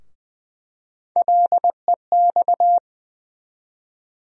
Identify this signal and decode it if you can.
Morse code, heard as a keyed tone, about 20 wpm, 700 Hz. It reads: LEX
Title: USA Call Sign Postfixes - 3 Letters 20wpm